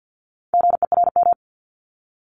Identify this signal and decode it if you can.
Morse code, heard as a keyed tone, about 50 wpm, 700 Hz. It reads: GERM
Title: Multi-letter Phonemes 50wpm